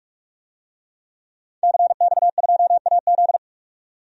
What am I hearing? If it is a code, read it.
Morse code, heard as a keyed tone, about 45 wpm, 700 Hz. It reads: CX2AZ
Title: DXCC-100 Call Signs - 5 Characters - 45wpm